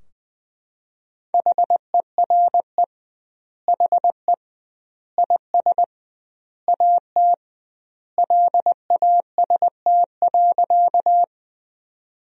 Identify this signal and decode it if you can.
Morse code, heard as a keyed tone, about 20 wpm, 700 Hz. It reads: HERE HE IS AT LAST.
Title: 2x Word-Spacing - Sentences from Top 300 Words - 20wpm